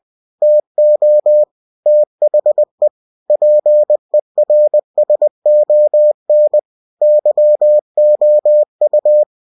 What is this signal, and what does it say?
Morse code, heard as a keyed tone, about 20 wpm, 600 Hz. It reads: TO THE PERSON YOU
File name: cw-books / winnie-the-pooh / winnie-20wpm-page-0119